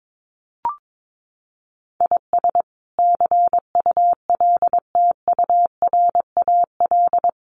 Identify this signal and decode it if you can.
Morse code, heard as a keyed tone, about 22 wpm, 700 Hz. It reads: IS CULTURAL
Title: Sentences from Top 1500 Words 22wpm